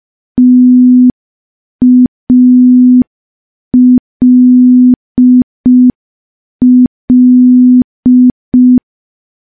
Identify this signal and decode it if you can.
Morse code, heard as a keyed tone, about 5 wpm, 250 Hz. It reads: TALL
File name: morse / morse-training-05WPM